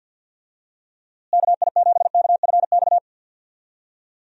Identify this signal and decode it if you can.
Morse code, heard as a keyed tone, about 51 wpm, 700 Hz. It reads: KI7KFX
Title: USA Call Signs - 6 Characters - Repeated 3x - 50wpm